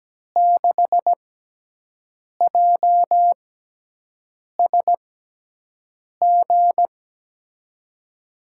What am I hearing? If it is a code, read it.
Morse code, heard as a keyed tone, about 17 wpm, spaced as for 6 wpm, 700 Hz. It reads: 6JSG
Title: USA Call Signs - 5 Characters - 6x Character Spacing - 17wpm